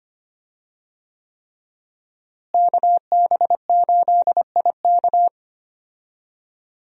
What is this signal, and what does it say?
Morse code, heard as a keyed tone, about 25 wpm, 700 Hz. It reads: KB8IK
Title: USA Call Signs - 5 Characters - Repeated 3x - 25wpm